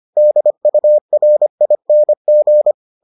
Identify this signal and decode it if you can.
Morse code, heard as a keyed tone, about 25 wpm, 600 Hz. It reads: DURING